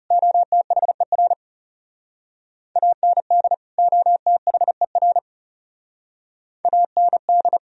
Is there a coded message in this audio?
Yes: Morse code from 0.1 to 7.6 s, heard as a keyed tone, about 39 wpm, 700 Hz. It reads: OTHER AND OTHER AND